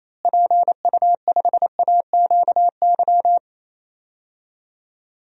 Morse code, heard as a keyed tone, about 28 wpm, 700 Hz. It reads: PU5AQY
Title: DXCC-100 Call Signs - 6 Characters - Repeated 2x - 28wpm